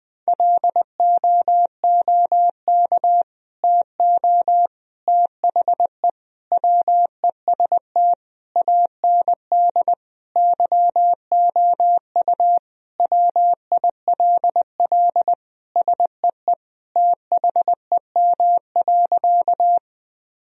Morse code, heard as a keyed tone, about 20 wpm, 700 Hz. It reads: LOOK TO THE WEST AND YOU WILL SEE THEM.
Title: Sentences from Top 1000 Words - Review Part 2 - 20wpm